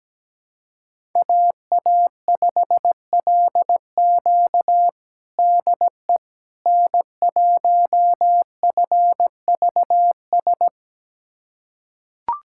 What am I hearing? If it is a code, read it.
Morse code, heard as a keyed tone, about 17 wpm, 700 Hz. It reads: AA5LQ DE N1FVS
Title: USA Callsign DE Callsign - 5 Characters - 17wpm